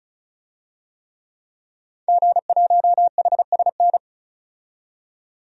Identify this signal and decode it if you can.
Morse code, heard as a keyed tone, about 36 wpm, 700 Hz. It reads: G1HSN
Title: DXCC-100 Call Signs - 5 Characters - 35wpm